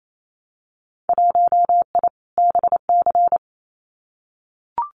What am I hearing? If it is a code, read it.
Morse code, heard as a keyed tone, about 28 wpm, 700 Hz. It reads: 1I BC
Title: Winter Field Day Exchanges Encore 28wpm